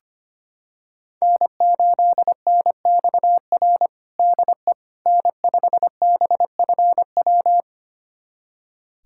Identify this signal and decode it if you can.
Morse code, heard as a keyed tone, about 25 wpm, 700 Hz. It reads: N8NXR DE N5BFW
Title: USA Callsign DE Callsign - 5 Characters - 25wpm